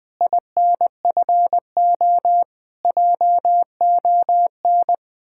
Morse code, heard as a keyed tone, about 20 wpm, 700 Hz. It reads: INFO JON